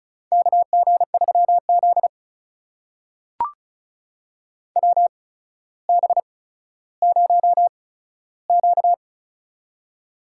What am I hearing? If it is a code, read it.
Morse code, heard as a keyed tone, about 36 wpm, spaced as for 15 wpm, 700 Hz. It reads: KG3Z WB0Q
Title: USA Call Signs - 4 Characters - 8x Character Spacing - 35wpm